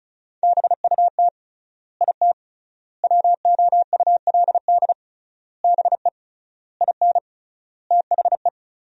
Morse code, heard as a keyed tone, about 36 wpm, 700 Hz. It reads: BUT IT WOULD BE IN THE
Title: 3x Word-Spacing - Sentences from Top 300 Words - 35wpm